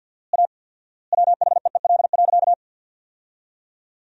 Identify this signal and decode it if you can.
Morse code, heard as a keyed tone, about 53 wpm, 700 Hz. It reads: A WHEEL.
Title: 4x Word-Spacing - Sentences from Top 500 Words - 50wpm